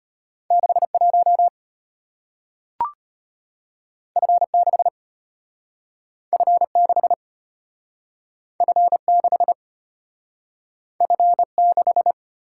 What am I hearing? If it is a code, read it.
Morse code, heard as a keyed tone, about 34 wpm, 700 Hz. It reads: 61 F6 F6 F6 F6